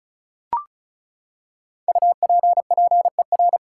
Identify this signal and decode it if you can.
Morse code, heard as a keyed tone, about 36 wpm, 700 Hz. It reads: UPPER
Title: Five Letter Words - 35wpm